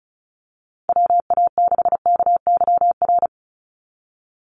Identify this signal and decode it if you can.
Morse code, heard as a keyed tone, about 34 wpm, 700 Hz. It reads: WA6KYR
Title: NA SOTA Chaser Callsigns 35wpm